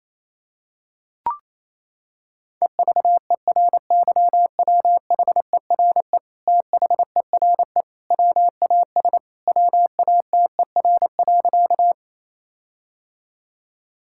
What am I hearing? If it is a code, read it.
Morse code, heard as a keyed tone, about 28 wpm, 700 Hz. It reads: EVERYWHERE THERE WAS WATER.
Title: Sentences from Top 1600 Words 28wpm